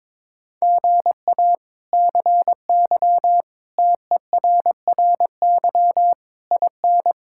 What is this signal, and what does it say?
Morse code, heard as a keyed tone, about 22 wpm, 700 Hz. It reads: GA CY TERRY IN
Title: K1USN SST QSO Practice - 22wpm